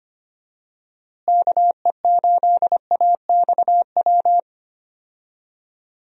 Morse code, heard as a keyed tone, about 25 wpm, 700 Hz. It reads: KE8AXW
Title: USA Call Signs - 6 Characters - Repeated 3x - 25wpm